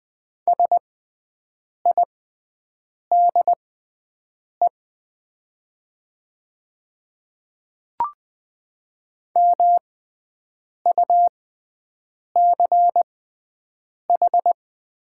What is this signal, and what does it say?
Morse code, heard as a keyed tone, about 20 wpm, spaced as for 7 wpm, 700 Hz. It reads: SIDE MUCH
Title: 6x Character Spacing - Top 300 Words - 20wpm